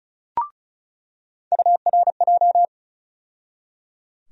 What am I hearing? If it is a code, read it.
Morse code, heard as a keyed tone, about 36 wpm, 700 Hz. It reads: URJ
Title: USA Call Sign Postfixes - 3 Letters 35wpm